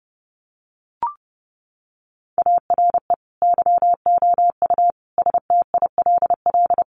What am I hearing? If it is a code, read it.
Morse code, heard as a keyed tone, about 30 wpm, 700 Hz. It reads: ARE YOU STILL
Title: Sentences from Top 400 Words 30wpm